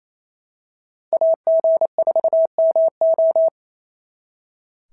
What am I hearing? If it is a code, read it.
Morse code, heard as a keyed tone, about 26 wpm, 650 Hz. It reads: AG4MO
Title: SOTA QSO Practice 28wpm